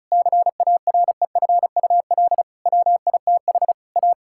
Morse code, heard as a keyed tone, about 36 wpm, 700 Hz. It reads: CAREFUL WITH A
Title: Sentences from Top 1600 Words 35wpm